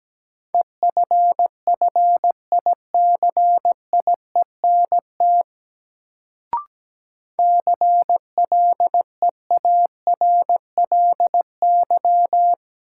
Morse code, heard as a keyed tone, about 17 wpm, 700 Hz. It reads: EFFICIENT CLEARLY
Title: Sentences from Top 1300 Words - New Word Drill Alt - 17wpm